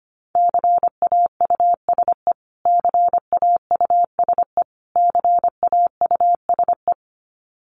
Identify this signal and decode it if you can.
Morse code, heard as a keyed tone, about 25 wpm, 700 Hz. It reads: CAUSE CAUSE CAUSE
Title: Top 500 Words Repeated 25wpm